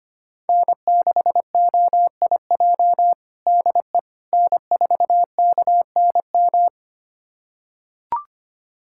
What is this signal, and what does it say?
Morse code, heard as a keyed tone, about 25 wpm, 700 Hz. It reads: N6OIJ DE N4KNM
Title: USA Callsign DE Callsign - 5 Characters - 25wpm